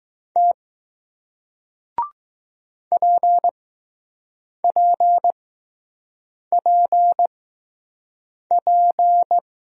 Morse code, heard as a keyed tone, about 20 wpm, spaced as for 6 wpm, 700 Hz. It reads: T PPPP